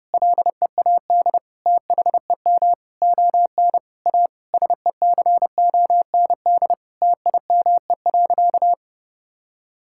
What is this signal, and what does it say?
Morse code, heard as a keyed tone, about 32 wpm, 700 Hz. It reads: LEAD THEM ON A SECOND TIME.
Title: Sentences from Top 1000 Words - Review Part 5 - 30wpm